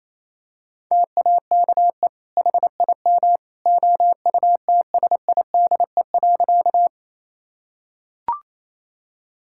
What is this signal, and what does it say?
Morse code, heard as a keyed tone, about 28 wpm, 700 Hz. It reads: TAKE HIM OUTSIDE.
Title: Sentences from Top 1000 Words - Review Part 6 - 28wpm